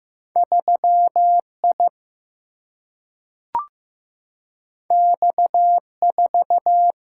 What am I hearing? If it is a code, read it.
Morse code, heard as a keyed tone, about 16 wpm, 700 Hz. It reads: 3I X4
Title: Single-Letter Single-Number 15wpm